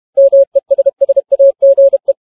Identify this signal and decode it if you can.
Morse code, heard as a keyed tone, about 31 wpm, 550 Hz. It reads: MESSAGE